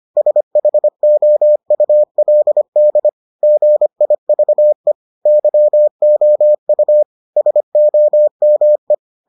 Morse code, heard as a keyed tone, about 25 wpm, 600 Hz. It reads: SHOULD GIVE YOU SOME